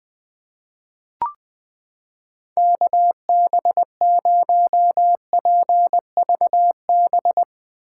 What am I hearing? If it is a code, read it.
Morse code, heard as a keyed tone, about 20 wpm, 700 Hz. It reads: KB0PVB
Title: 2023 POTA Activator Call Signs - 20wpm